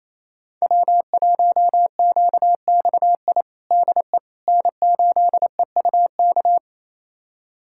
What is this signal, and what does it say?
Morse code, heard as a keyed tone, about 27 wpm, 700 Hz. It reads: W1QXI DE N8EUK
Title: USA Callsign DE Callsign - 5 Characters - 28wpm